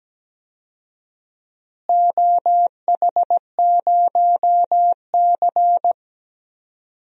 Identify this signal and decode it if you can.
Morse code, heard as a keyed tone, about 17 wpm, 700 Hz. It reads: OH0C
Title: DXCC-100 Call Signs - 4 Characters - Repeated 2x - 17wpm